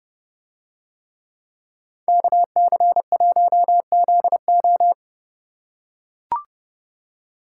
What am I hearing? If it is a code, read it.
Morse code, heard as a keyed tone, about 30 wpm, 700 Hz. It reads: KC1ZO